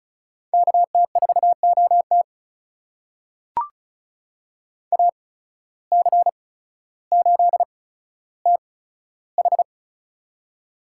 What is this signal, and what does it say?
Morse code, heard as a keyed tone, about 36 wpm, spaced as for 15 wpm, 700 Hz. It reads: KT4OT AC8TH
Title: USA Call Signs - 5 Characters - 8x Character Spacing - 35wpm